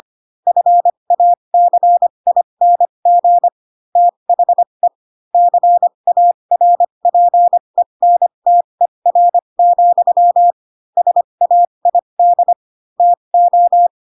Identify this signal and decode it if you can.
Morse code, heard as a keyed tone, about 25 wpm, spaced as for 22 wpm, 700 Hz. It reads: FACING THE CARPENTER, SAID TO
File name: Pinocchio / Pinocchio ch2 18 25 ik1hge